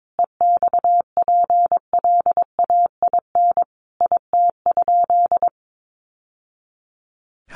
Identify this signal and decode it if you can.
Morse code, heard as a keyed tone, about 22 wpm, 700 Hz. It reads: EXPLAIN IT?
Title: Sentences from Top 1100 Words 22wpm